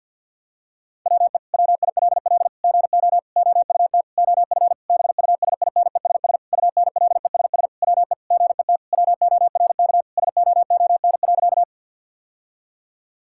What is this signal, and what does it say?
Morse code, heard as a keyed tone, about 50 wpm, 700 Hz. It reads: WE WILL GO OUT OF BUSINESS UNLESS WE GET WORK SOON.
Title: Sentences from Top 1000 Words - Review Part 6 - 50wpm